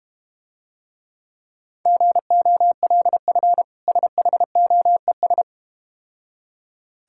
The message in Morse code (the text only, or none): GOLF SHOES